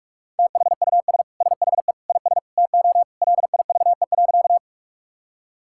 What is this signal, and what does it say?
Morse code, heard as a keyed tone, about 41 wpm, 700 Hz. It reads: THUS SHE IS TO LIVE.
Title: Sentences from Top 1000 Words - Review Part 5 - 45wpm